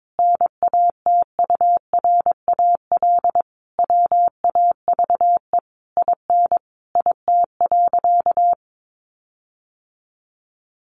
Morse code, heard as a keyed tone, about 22 wpm, 700 Hz. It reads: NATURAL WAVE IN IT.
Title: Sentences from Top 1400 Words 22wpm